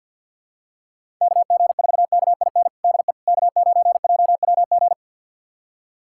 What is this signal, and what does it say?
Morse code, heard as a keyed tone, about 50 wpm, 700 Hz. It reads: KG4KIN DE K9JWG
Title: USA Callsign DE Callsign 50wpm